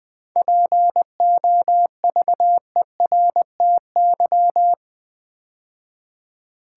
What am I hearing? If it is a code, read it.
Morse code, heard as a keyed tone, about 20 wpm, 700 Hz. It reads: POVERTY
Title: 7 Letter Words - First 3 Letters Given - 20wpm